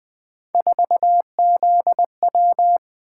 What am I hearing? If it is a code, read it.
Morse code, heard as a keyed tone, about 20 wpm, 700 Hz. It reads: 4ZW
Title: Three Letter-Number Combinations - 20wpm